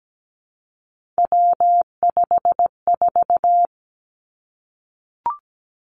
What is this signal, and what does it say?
Morse code, heard as a keyed tone, about 17 wpm, 700 Hz. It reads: W54
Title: Three Challenging Letter-Number Combinations 17wpm